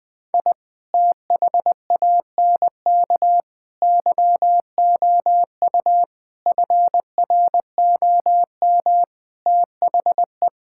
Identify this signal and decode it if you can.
Morse code, heard as a keyed tone, about 20 wpm, 700 Hz. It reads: I THANK YOU FROM THE
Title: Sentences with 8-Word Phrases - 20wpm